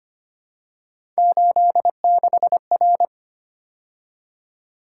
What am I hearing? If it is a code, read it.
Morse code, heard as a keyed tone, about 25 wpm, 700 Hz. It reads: 86R
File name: Three Letter-Number Combinations - 25wpm